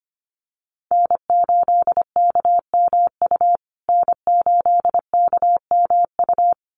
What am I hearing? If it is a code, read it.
Morse code, heard as a keyed tone, about 25 wpm, 700 Hz. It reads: N8KMU N8KMU
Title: USA Call Signs Repeated 2x 25wpm